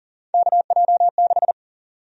Morse code, heard as a keyed tone, about 40 wpm, 700 Hz. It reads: KJ6
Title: USA Call Sign Prefixes - 40wpm